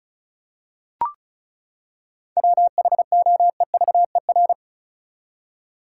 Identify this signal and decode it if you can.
Morse code, heard as a keyed tone, about 36 wpm, 700 Hz. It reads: WHOEVER
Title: Seven Letter Words - 35wpm